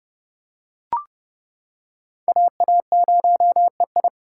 Morse code, heard as a keyed tone, about 30 wpm, 700 Hz. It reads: AA0EI